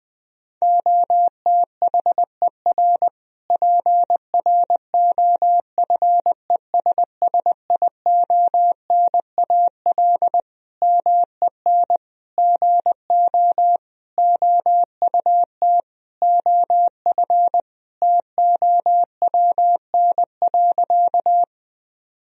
Morse code, heard as a keyed tone, about 20 wpm, 700 Hz. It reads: OTHER PROFESSIONAL MEN GO OUT OF TOWN.
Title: Sentences from Top 1000 Words - Review Part 4 - 20wpm